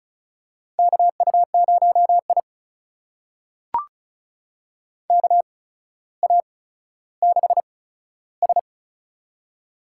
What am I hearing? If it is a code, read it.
Morse code, heard as a keyed tone, about 34 wpm, spaced as for 15 wpm, 700 Hz. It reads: KU0I KA6S
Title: USA Call Signs - 4 Characters - 8x Character Spacing - 35wpm